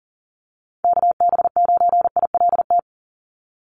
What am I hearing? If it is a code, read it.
Morse code, heard as a keyed tone, about 40 wpm, 700 Hz. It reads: KB9ILT
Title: NA SOTA Chaser Callsigns 40wpm